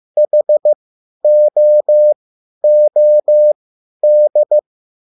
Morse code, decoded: HOOD

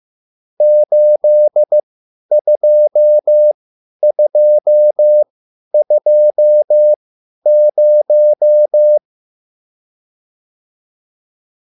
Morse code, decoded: 82220

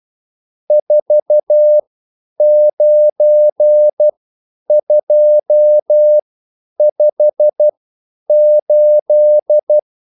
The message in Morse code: 49258

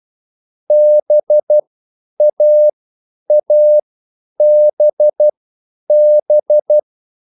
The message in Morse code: BAABB

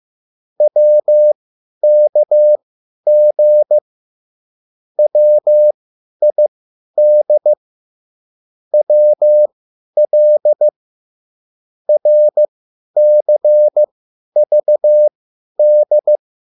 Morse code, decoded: WKG WID WL RCVD

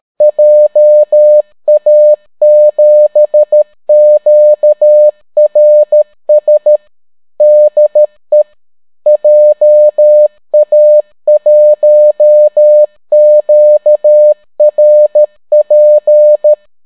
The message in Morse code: JA7QRS DE JA1QRP

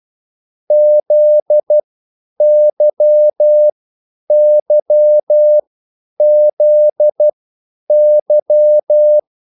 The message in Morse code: ZYYZY

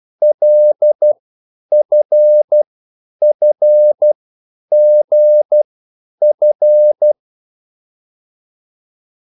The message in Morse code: LFFGF

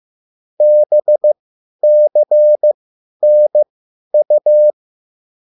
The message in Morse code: BCNU

